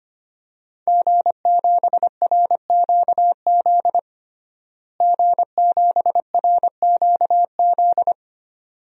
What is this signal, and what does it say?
Morse code, heard as a keyed tone, about 25 wpm, 700 Hz. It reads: G7RQZ G7RQZ
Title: DXCC-100 Call Signs - 5 Characters - Repeated 2x - 25wpm